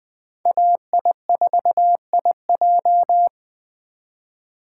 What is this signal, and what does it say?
Morse code, heard as a keyed tone, about 20 wpm, 700 Hz. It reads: AI4IJ
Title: Speed Racing - USA 2x2 Call Signs 17wpm